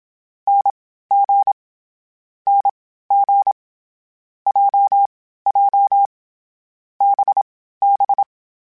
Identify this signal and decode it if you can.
Morse code, heard as a keyed tone, about 27 wpm, spaced as for 15 wpm, 800 Hz. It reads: NG NG JJ BB